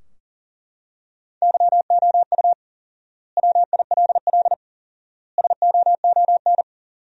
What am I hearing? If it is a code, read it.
Morse code, heard as a keyed tone, about 40 wpm, 700 Hz. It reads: YOU WILL SOON
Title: 4x Word-Spacing - Sentences from Top 400 Words - 40wpm